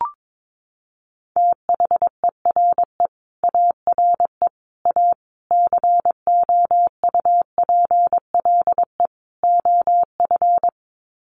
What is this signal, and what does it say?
Morse code, heard as a keyed tone, about 22 wpm, 700 Hz. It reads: THERE ARE A COUPLE OF
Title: Sets of 5 Words Encore 22wpm